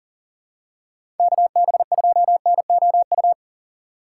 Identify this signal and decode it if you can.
Morse code, heard as a keyed tone, about 37 wpm, 700 Hz. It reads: KB2NOU
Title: USA Call Signs - 3x Character Spacing - 40wpm